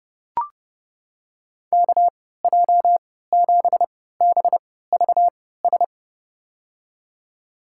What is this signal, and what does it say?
Morse code, heard as a keyed tone, about 30 wpm, spaced as for 17 wpm, 700 Hz. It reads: KJ7BVS
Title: USA Call Signs - 6 Characters - 3x Character Spacing - 30wpm